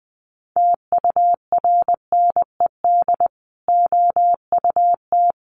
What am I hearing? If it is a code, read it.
Morse code, heard as a keyed tone, about 20 wpm, 700 Hz. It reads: TURNED OUT
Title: Sets of 5 Words 20wpm